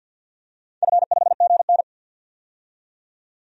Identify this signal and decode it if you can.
Morse code, heard as a keyed tone, about 51 wpm, 700 Hz. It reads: F5GN
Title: DXCC-100 Call Signs - 4 Characters - 50wpm